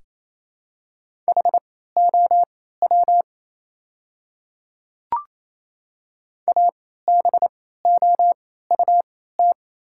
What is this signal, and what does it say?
Morse code, heard as a keyed tone, about 28 wpm, spaced as for 16 wpm, 690 Hz. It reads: HOW ABOUT
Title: 3x Character Spacing - Top 100 Words - 28wpm